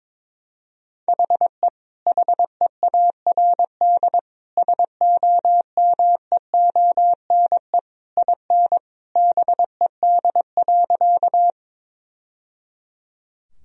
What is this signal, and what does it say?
Morse code, heard as a keyed tone, about 22 wpm, 700 Hz. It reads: HE HEARD SOMEONE IN BED.
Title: Sentences from Top 1000 Words - Review Part 4 - 22wpm